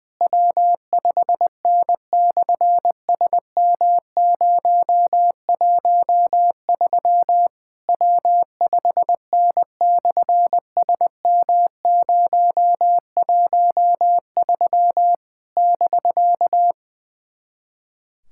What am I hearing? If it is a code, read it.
Morse code, heard as a keyed tone, about 20 wpm, 700 Hz. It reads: W5N/SM013 W5N/SM013 <BK>
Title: SOTA QSO Practice 20wpm